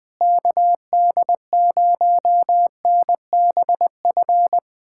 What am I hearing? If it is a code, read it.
Morse code, heard as a keyed tone, about 20 wpm, 700 Hz. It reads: KD0NBF
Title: K1USN SST QSO Practice - 20wpm